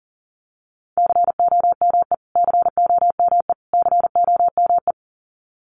Morse code, heard as a keyed tone, about 40 wpm, 700 Hz. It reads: COME COME COME
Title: Top 500 Words Repeated 40wpm